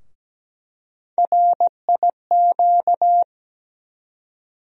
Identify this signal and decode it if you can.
Morse code, heard as a keyed tone, about 17 wpm, 700 Hz. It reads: RIQ
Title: USA Call Sign Postfixes - 3 Letters 17wpm